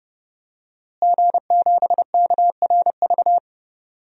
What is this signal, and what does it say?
Morse code, heard as a keyed tone, about 32 wpm, 700 Hz. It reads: G7KRV
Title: DXCC-100 Call Signs - 5 Characters - Repeated 2x - 30wpm